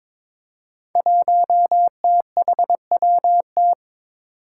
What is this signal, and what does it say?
Morse code, heard as a keyed tone, about 21 wpm, 700 Hz. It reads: 1THWT
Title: Five Challenging Letter-Number Combinations 22wpm